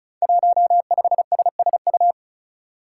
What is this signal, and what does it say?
Morse code, heard as a keyed tone, about 36 wpm, 700 Hz. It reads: 15SSU
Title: Five Challenging Letter-Number Combinations 35wpm